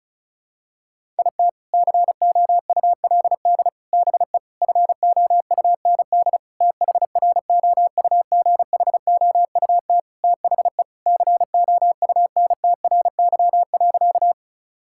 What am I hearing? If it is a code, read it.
Morse code, heard as a keyed tone, about 34 wpm, 700 Hz. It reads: IT COULD BE FOUND THROUGHOUT THE COUNTRY.
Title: Sentences from Top 1000 Words - Review Part 5 - 35wpm